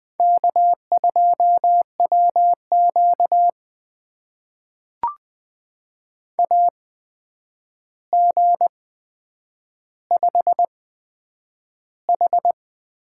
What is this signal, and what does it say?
Morse code, heard as a keyed tone, about 20 wpm, spaced as for 9 wpm, 700 Hz. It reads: K2WQ AG5H
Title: USA Call Signs - 4 Characters - 8x Character Spacing - 20wpm